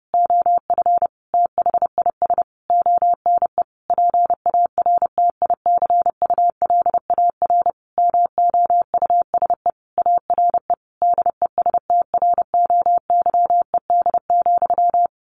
Morse code, heard as a keyed tone, about 30 wpm, 700 Hz. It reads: OF THIS ONE PARTICULAR MOUSE ARE DESTROYED,
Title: A Sound Of Thunder 30wpm